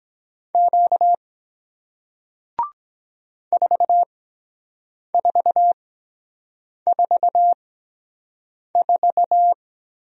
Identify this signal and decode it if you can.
Morse code, heard as a keyed tone, about 23 wpm, spaced as for 7 wpm, 700 Hz. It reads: Q 4444